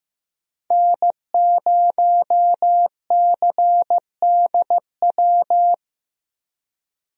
Morse code, heard as a keyed tone, about 15 wpm, 700 Hz. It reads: N0CDW
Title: USA Call Signs - 5 Characters - Repeated 2x - 15wpm